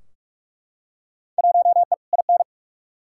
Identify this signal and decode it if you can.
Morse code, heard as a keyed tone, about 41 wpm, 700 Hz. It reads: 1E IN
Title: Field Day Exchanges - 45wpm